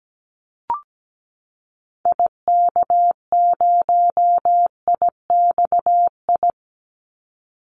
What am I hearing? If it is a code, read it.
Morse code, heard as a keyed tone, about 17 wpm, 700 Hz. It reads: IK0IXI
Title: Call Signs Repeated 3x - 17wpm